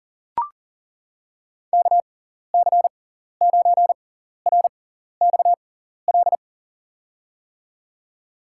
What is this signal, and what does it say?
Morse code, heard as a keyed tone, about 40 wpm, spaced as for 14 wpm, 700 Hz. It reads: KC9RXL
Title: USA Call Signs - 6 Characters - 6x Character Spacing - 40wpm